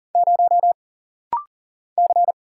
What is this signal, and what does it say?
Morse code, heard as a keyed tone, about 40 wpm, 700 Hz. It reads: 0 C
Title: Single Letter-Number Alt - Rapid-Fire 40wpm